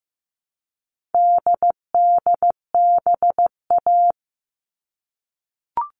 Morse code, heard as a keyed tone, about 15 wpm, 700 Hz. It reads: DDBA
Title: All Cut Numbers - 4-Digit 15wpm